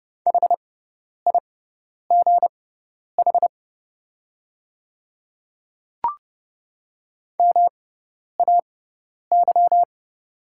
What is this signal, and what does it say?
Morse code, heard as a keyed tone, about 30 wpm, spaced as for 10 wpm, 690 Hz. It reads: HIGH MAY
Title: 6x Character Spacing - Top 200 Words - 30wpm